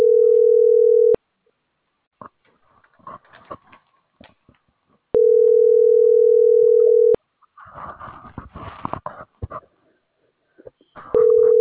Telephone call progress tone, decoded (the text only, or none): ringback tone